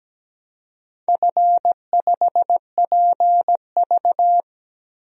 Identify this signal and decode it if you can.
Morse code, heard as a keyed tone, about 17 wpm, 700 Hz. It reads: F5PV
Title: DXCC-100 Call Signs - 4 Characters - Repeated 2x - 17wpm